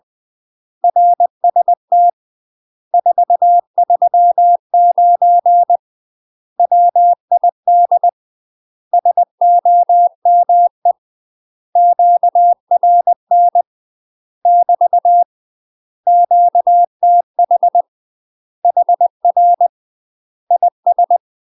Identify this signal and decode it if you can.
Morse code, heard as a keyed tone, about 20 wpm, 700 Hz. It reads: RST 439 WID SOME QRN <BT> QTH HR IS
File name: UK MASTER QSOs TRAINING / 20WPM / UK MASTER QSOs 0073